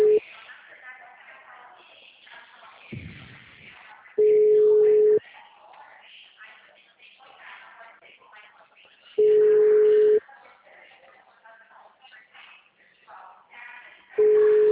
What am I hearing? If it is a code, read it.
Ringback tone (Europe, 425 Hz), with speech.